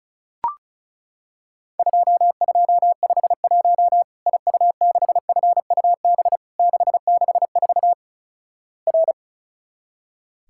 Morse code, heard as a keyed tone, about 36 wpm, 700 Hz. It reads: 2251 IU6FUB 664 R
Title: Worked All Europe DX Contest - QTC Exchange - 35wpm